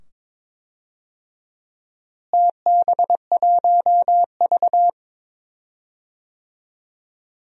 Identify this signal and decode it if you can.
Morse code, heard as a keyed tone, about 22 wpm, 700 Hz. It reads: TB1V
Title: DXCC-100 Call Signs - 4 Characters - Repeated 2x - 22wpm